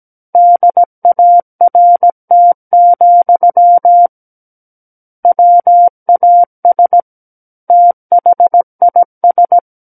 Morse code, heard as a keyed tone, about 17 wpm, 700 Hz. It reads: DART, WAS THIS
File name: Book One / 08-Chapter 8 - 17 WPM